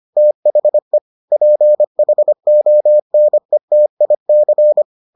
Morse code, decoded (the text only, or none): THE PHONETIC